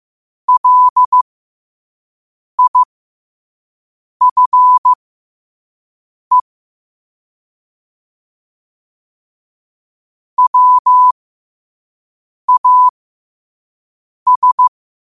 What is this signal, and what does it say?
Morse code, heard as a keyed tone, about 15 wpm, spaced as for 5 wpm, 1000 Hz. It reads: LIFE WAS
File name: qotdfiles / 5wpm